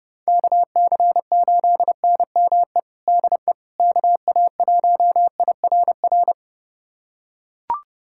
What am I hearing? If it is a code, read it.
Morse code, heard as a keyed tone, about 29 wpm, 700 Hz. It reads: KC8NME DE KA1IRR
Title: USA Callsign DE Callsign - 6 Characters - 30wpm